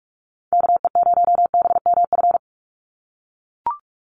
Morse code, heard as a keyed tone, about 44 wpm, 700 Hz. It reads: KE0BMF
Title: USA Call Signs 45wpm